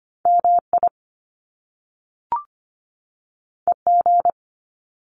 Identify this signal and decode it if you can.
Morse code, heard as a keyed tone, about 25 wpm, 700 Hz. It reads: MI EG
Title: Top Two Letter Combinations in Words 25wpm